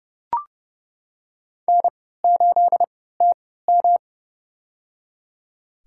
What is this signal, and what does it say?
Morse code, heard as a keyed tone, about 30 wpm, spaced as for 17 wpm, 700 Hz. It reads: N8TM
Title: USA Call Signs - 4 Characters - 3x Character Spacing - 30wpm